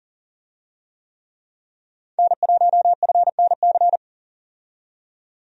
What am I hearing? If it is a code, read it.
Morse code, heard as a keyed tone, about 40 wpm, 700 Hz. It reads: N1FNC
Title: USA Call Signs - 5 Characters - 40wpm